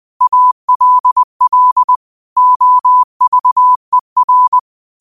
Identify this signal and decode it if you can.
Morse code, heard as a keyed tone, about 20 wpm, 1000 Hz. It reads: ALL OVER